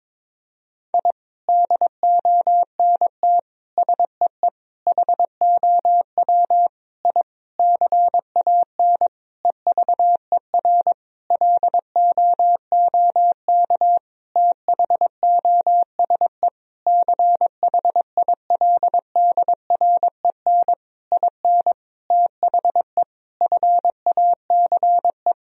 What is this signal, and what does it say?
Morse code, heard as a keyed tone, about 22 wpm, 700 Hz. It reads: I DONT SEE HOW I CAN EVER LOOK THOSE CHILDREN IN THE FACE
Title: Sentences from Top 1000 Words - Review Part 2 - 22wpm